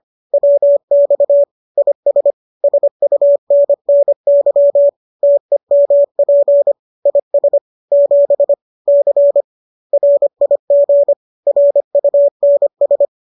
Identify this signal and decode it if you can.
Morse code, heard as a keyed tone, about 25 wpm, 570 Hz. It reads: WX IS SUNNY TEMP IS 7 C RIG RUNS